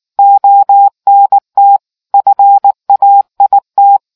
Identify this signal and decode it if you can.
Morse code, heard as a keyed tone, about 19 wpm, 800 Hz. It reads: ONT FAIT